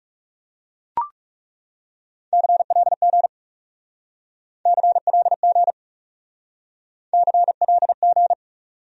Morse code, heard as a keyed tone, about 38 wpm, 700 Hz. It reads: CLG CLG CLG
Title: Speed-Racing Abbreviations - 30wpm